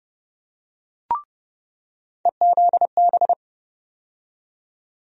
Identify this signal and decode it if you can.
Morse code, heard as a keyed tone, about 30 wpm, 700 Hz. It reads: EZB
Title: USA Call Sign Postfixes - 3 Letters 30wpm